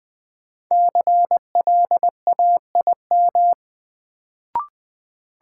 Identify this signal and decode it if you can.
Morse code, heard as a keyed tone, about 20 wpm, 700 Hz. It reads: CLAIM
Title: Sentences from Top 1500 Words - New Word Drill Alt - 20wpm